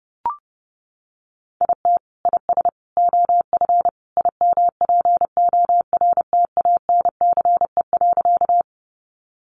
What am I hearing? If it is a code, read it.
Morse code, heard as a keyed tone, about 30 wpm, 700 Hz. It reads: IT IS OF IMPORTANCE.
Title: Sentences from Top 1200 Words 30wpm